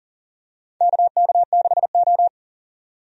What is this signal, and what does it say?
Morse code, heard as a keyed tone, about 40 wpm, 700 Hz. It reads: KK6O
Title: USA Call Signs - 4 Characters - 40wpm